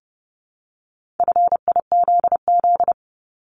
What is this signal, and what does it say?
Morse code, heard as a keyed tone, about 30 wpm, 700 Hz. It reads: FIZZ
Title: Multi-letter Phonemes 30wpm